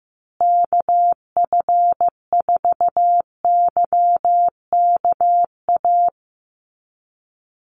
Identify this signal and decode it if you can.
Morse code, heard as a keyed tone, about 15 wpm, 700 Hz. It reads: KF4YKA
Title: USA Call Signs Repeated 2x 15wpm